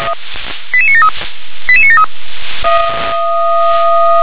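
Five-tone selective call (ZVEI), heard as quick runs of five tones, tones of about 80 ms. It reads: 88973 88973
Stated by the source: ZVEI five-tone selective call received on an SDR.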